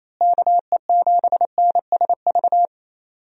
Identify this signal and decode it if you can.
Morse code, heard as a keyed tone, about 28 wpm, 700 Hz. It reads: KE7NSV
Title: USA Call Signs - 6 Characters - Repeated 2x - 28wpm